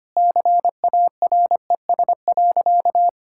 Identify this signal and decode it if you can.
Morse code, heard as a keyed tone, about 25 wpm, 700 Hz. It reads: CARES.
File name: practice / 25wpm / Sentences with 3-Word Phrases - 25wpm